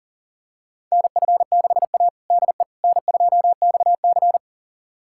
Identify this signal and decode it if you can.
Morse code, heard as a keyed tone, about 40 wpm, 700 Hz. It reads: NF6A DE N2XC
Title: USA Callsign DE Callsign - 4 Characters - 40wpm